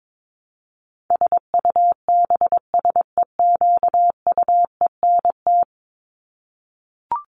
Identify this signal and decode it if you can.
Morse code, heard as a keyed tone, about 22 wpm, 700 Hz. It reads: SUBSEQUENT
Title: LY Suffix - Top 100 Words with Base Words 22wpm